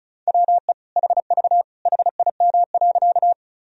Morse code, heard as a keyed tone, about 36 wpm, 700 Hz. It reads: WE HV HIM.
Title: Sentences from Top 100 Words and Abbreviations 35wpm